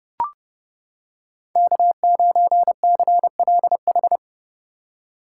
Morse code, heard as a keyed tone, about 31 wpm, 700 Hz. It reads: K9CLH